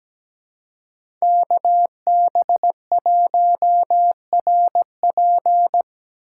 Morse code, heard as a keyed tone, about 20 wpm, 700 Hz. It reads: KB1RP
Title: Speed Racing - USA 2x2 Call Signs 17wpm